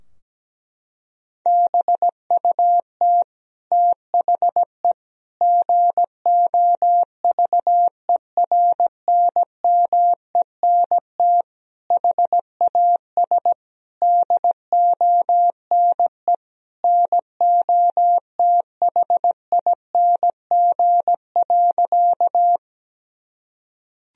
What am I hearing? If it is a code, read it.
Morse code, heard as a keyed tone, about 17 wpm, 700 Hz. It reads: BUT THE GOVERNMENT HAS DONE NOTHING.
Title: Sentences from Top 1000 Words - Review Part 3 - 17wpm